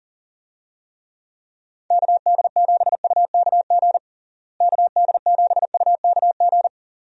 Morse code, heard as a keyed tone, about 40 wpm, 700 Hz. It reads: KD7UKG KD7UKG
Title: USA Call Signs - 6 Characters - Repeated 2x - 40wpm